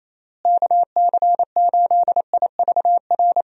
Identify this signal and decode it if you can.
Morse code, heard as a keyed tone, about 28 wpm, 700 Hz. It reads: KC8IVR
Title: USA Call Signs - 6 Characters - Repeated 2x - 28wpm